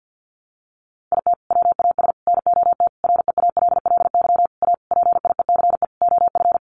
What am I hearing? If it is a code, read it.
Morse code, heard as a keyed tone, about 48 wpm, 700 Hz. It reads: IT WAS NOT REALLY A PIECE OF
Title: Sentences from Top 600 Words 50wpm